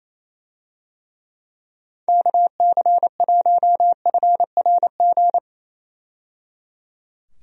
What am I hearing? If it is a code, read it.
Morse code, heard as a keyed tone, about 28 wpm, 700 Hz. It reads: KC1FRG
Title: USA Call Signs - 6 Characters - 28wpm